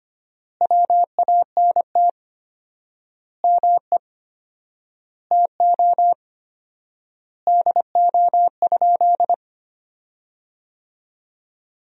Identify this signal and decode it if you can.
Morse code, heard as a keyed tone, about 25 wpm, 700 Hz. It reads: WANT ME TO DO?
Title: 4x Word-Spacing - Sentences from Top 200 Words - 25wpm